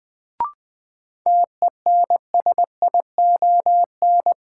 Morse code, heard as a keyed tone, about 20 wpm, 700 Hz. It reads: TENSION
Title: Sentences from Top 1600 Words - New Word Drill Alt - 20wpm